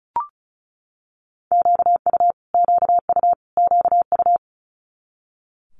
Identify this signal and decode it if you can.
Morse code, heard as a keyed tone, about 36 wpm, 700 Hz. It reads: QU QU QU
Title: Multi-letter Phonemes Repeated 3x 35wpm